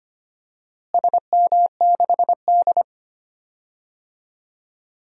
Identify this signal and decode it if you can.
Morse code, heard as a keyed tone, about 25 wpm, 700 Hz. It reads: SM6D
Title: DXCC-100 Call Signs - 4 Characters - Repeated 2x - 25wpm